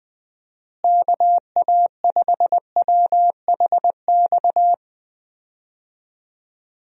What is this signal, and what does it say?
Morse code, heard as a keyed tone, about 20 wpm, 700 Hz. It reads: KA5WHX
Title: USA Call Signs - 6 Characters - Repeated 3x - 20wpm